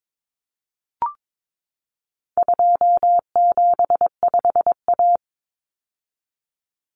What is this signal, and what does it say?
Morse code, heard as a keyed tone, about 22 wpm, 700 Hz. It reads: 275A
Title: Common Cut Numbers - 4-Digit 22wpm